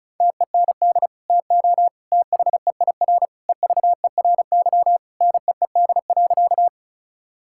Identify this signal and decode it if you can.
Morse code, heard as a keyed tone, about 36 wpm, 700 Hz. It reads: TEND TO THEIR EVERY NEED.
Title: Sentences from Top 1000 Words - Review Part 5 - 35wpm